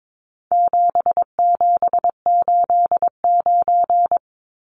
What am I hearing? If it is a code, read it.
Morse code, heard as a keyed tone, about 22 wpm, 700 Hz. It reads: 7789
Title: Numbers - 4-Digit 22wpm